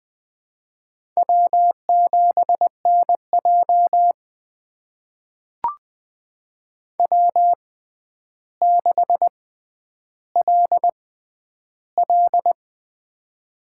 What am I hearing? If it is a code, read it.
Morse code, heard as a keyed tone, about 20 wpm, spaced as for 10 wpm, 700 Hz. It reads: W7NJ W6LL
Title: USA Call Signs - 6x Character Spacing - 20wpm